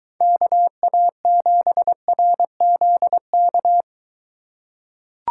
Morse code, heard as a keyed tone, about 22 wpm, 700 Hz. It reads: KA7RZK